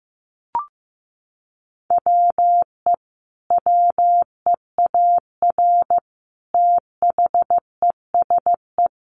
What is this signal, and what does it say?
Morse code, heard as a keyed tone, about 15 wpm, 700 Hz. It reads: WE WEAR THESE
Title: A Sound Of Thunder 15wpm